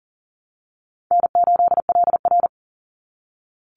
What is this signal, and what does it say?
Morse code, heard as a keyed tone, about 42 wpm, 700 Hz. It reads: N8LR
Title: Easy Call Signs - 40wpm